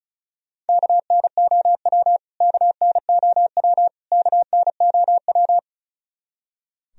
Morse code, heard as a keyed tone, about 34 wpm, 700 Hz. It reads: KNOW KNOW KNOW
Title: Top 100 Words Repeated 35wpm